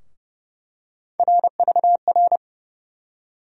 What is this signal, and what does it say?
Morse code, heard as a keyed tone, about 31 wpm, 700 Hz. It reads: RVR
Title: USA Call Sign Postfixes - 3 Letters 30wpm